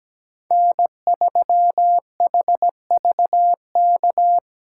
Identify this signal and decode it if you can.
Morse code, heard as a keyed tone, about 19 wpm, 700 Hz. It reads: N3HVK
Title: Speed Racing - USA Call Signs 17wpm